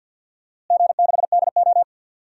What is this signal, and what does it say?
Morse code, heard as a keyed tone, about 47 wpm, 700 Hz. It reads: G6DO
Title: DXCC-100 Call Signs - 4 Characters - Repeated 2x - 50wpm